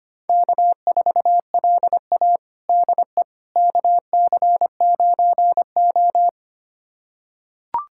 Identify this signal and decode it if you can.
Morse code, heard as a keyed tone, about 25 wpm, 700 Hz. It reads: K4LA DE KC9O
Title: USA Callsign DE Callsign - 4 Characters - 25wpm